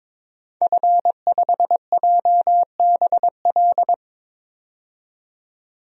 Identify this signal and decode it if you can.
Morse code, heard as a keyed tone, about 22 wpm, 700 Hz. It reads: F5JBL
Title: DXCC-100 Call Signs - 5 Characters - 22wpm